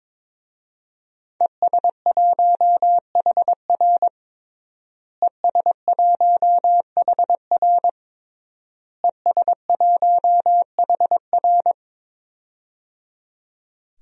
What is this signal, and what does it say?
Morse code, heard as a keyed tone, about 22 wpm, 700 Hz. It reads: ES1HR ES1HR ES1HR
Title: DXCC-100 Call Signs - 5 Characters - Repeated 3x - 22wpm